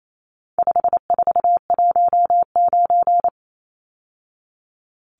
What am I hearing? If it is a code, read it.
Morse code, heard as a keyed tone, about 28 wpm, 700 Hz. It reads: 5419
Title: Numbers - 4-Digit 28wpm